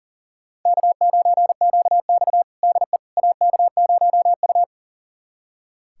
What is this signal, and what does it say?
Morse code, heard as a keyed tone, about 40 wpm, 700 Hz. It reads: K9QX DE AK0U
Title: USA Callsign DE Callsign - 4 Characters - 40wpm